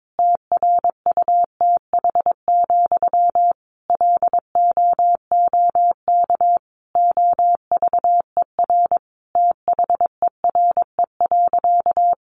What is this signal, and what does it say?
Morse code, heard as a keyed tone, about 22 wpm, 700 Hz. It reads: TRUTH, LOOK OVER THERE.
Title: Sentences from Top 800 Words 22wpm